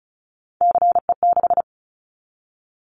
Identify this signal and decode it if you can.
Morse code, heard as a keyed tone, about 34 wpm, 700 Hz. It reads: CE6
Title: Callsign Prefixes for Top 100 DXCC - 35wpm